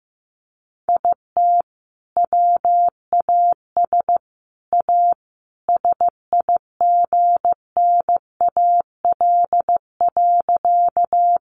Morse code, heard as a keyed tone, about 15 wpm, 700 Hz. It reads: IT WAS A SIGNAL.
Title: Sentences from Top 1400 Words 15wpm